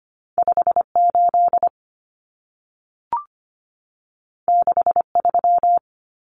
Morse code, heard as a keyed tone, about 25 wpm, 700 Hz. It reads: 58 63
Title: Common Cut Numbers - 2-Digit 25wpm